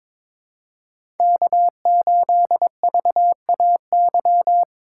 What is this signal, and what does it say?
Morse code, heard as a keyed tone, about 21 wpm, 700 Hz. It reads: K8VAY
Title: USA Call Signs - 5 Characters - 2x Character Spacing - 22wpm